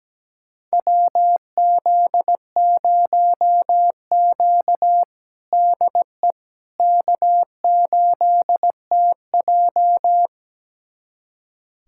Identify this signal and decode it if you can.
Morse code, heard as a keyed tone, about 17 wpm, 700 Hz. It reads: WZ0Q DE K8TJ
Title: USA Callsign DE Callsign - 4 Characters - 17wpm